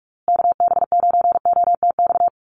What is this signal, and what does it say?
Morse code, heard as a keyed tone, about 44 wpm, 700 Hz. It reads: KB9OTX
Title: USA Call Signs Repeated 2x 45wpm